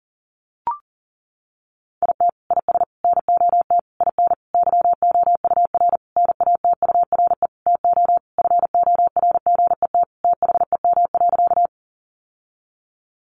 Morse code, heard as a keyed tone, about 40 wpm, 700 Hz. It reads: IT IS NOT IN YOUR NATURE TO FORGET THEM.
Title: Sentences from Top 1200 Words 40wpm